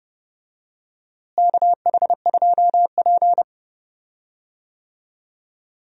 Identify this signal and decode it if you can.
Morse code, heard as a keyed tone, about 30 wpm, 700 Hz. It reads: KH2P
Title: USA Call Signs - 4 Characters - Repeated 2x - 30wpm